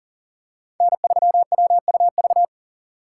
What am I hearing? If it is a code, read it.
Morse code, heard as a keyed tone, about 40 wpm, 700 Hz. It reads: N3WUV
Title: USA Call Signs - 5 Characters - 4x Character Spacing - 40wpm